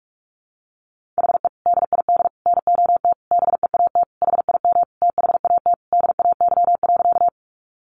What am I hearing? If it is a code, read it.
Morse code, heard as a keyed tone, about 44 wpm, 700 Hz. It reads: HE DID NOT BEAT HIM THAT DAY.
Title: Sentences from Top 1300 Words 45wpm